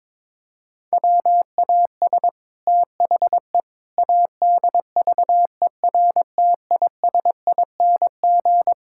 Morse code, heard as a keyed tone, about 21 wpm, 700 Hz. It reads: WAS THE ADVERTISING
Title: Sentences from Top 1000 Words - Review Part 6 - 22wpm